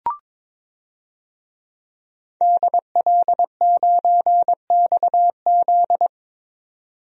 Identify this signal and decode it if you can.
Morse code, heard as a keyed tone, about 22 wpm, 700 Hz. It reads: DL9XZ
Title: DXCC-100 Call Signs - 5 Characters - 22wpm